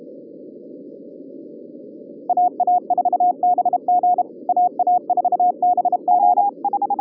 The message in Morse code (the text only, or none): AA4BG AA4BG